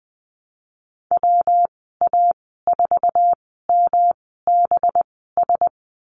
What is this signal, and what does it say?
Morse code, heard as a keyed tone, about 20 wpm, spaced as for 15 wpm, 700 Hz. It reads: WA4MBS
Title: 186 Lesson 30g - Letter b - Call Signs 20wpm